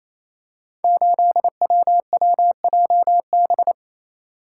Morse code, heard as a keyed tone, about 27 wpm, 700 Hz. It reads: 8WWJB
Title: Code Groups - 5 Letter:Numbers - 28wpm